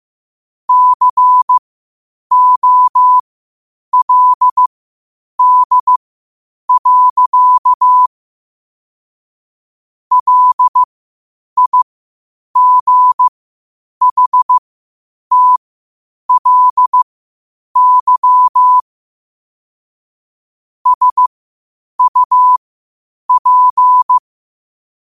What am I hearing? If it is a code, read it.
Morse code, heard as a keyed tone, about 15 wpm, spaced as for 8 wpm, 1000 Hz. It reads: COLD. LIGHTLY SUP
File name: qotdfiles / 8wpm